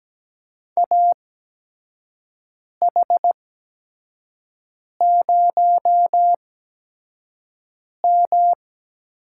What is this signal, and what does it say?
Morse code, heard as a keyed tone, about 17 wpm, spaced as for 5 wpm, 700 Hz. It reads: AH0M
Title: USA Call Signs - 4 Characters - 8x Character Spacing - 17wpm